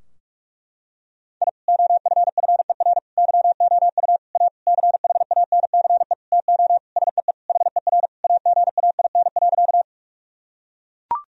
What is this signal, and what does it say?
Morse code, heard as a keyed tone, about 41 wpm, 700 Hz. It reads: I OFFER YOU A CHANCE TO SEE HER AGAIN.
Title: Sentences from Top 1000 Words - Review Part 4 - 45wpm